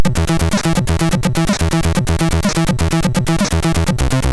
There is no code.